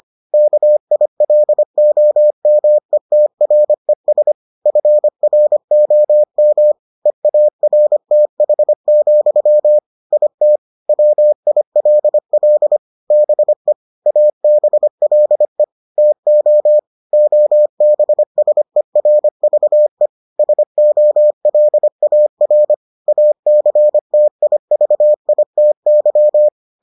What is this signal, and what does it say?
Morse code, heard as a keyed tone, about 25 wpm, 600 Hz. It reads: KILOMETRES FROM EARTH, IT WILL BE ABLE TO OBSERVE SOLAR ACTIVITY